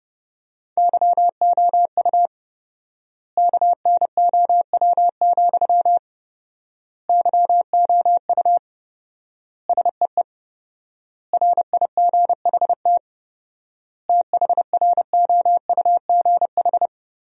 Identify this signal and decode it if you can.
Morse code, heard as a keyed tone, about 30 wpm, 700 Hz. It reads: YOU KNOW, YOU SEE RIGHT THROUGH
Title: 4x Word-Spacing - Sentences from Top 200 Words - 30wpm